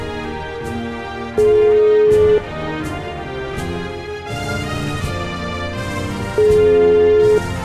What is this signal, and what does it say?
Ringback tone (Europe, 425 Hz), with noise.